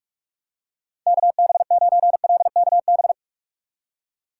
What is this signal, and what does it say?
Morse code, heard as a keyed tone, about 41 wpm, 700 Hz. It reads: KB9LKB
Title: USA Call Signs - 6 Characters - 4x Character Spacing - 45wpm